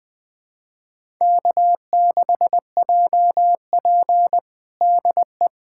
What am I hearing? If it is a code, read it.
Morse code, heard as a keyed tone, about 20 wpm, 700 Hz. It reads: K6JP DE
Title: USA Callsign DE Callsign - 4 Characters - 20wpm